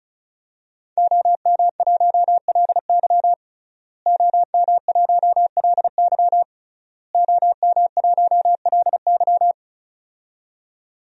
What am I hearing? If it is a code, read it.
Morse code, heard as a keyed tone, about 34 wpm, 700 Hz. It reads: OM1LY OM1LY OM1LY
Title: DXCC-100 Call Signs - 5 Characters - Repeated 3x - 35wpm